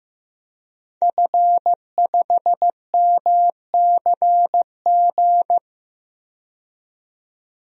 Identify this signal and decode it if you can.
Morse code, heard as a keyed tone, about 16 wpm, 700 Hz. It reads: F5MCG
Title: DXCC-100 Call Signs - 5 Characters - 15wpm